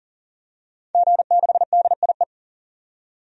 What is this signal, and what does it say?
Morse code, heard as a keyed tone, about 40 wpm, 700 Hz. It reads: G6DIE
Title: DXCC-100 Call Signs - 5 Characters - 40wpm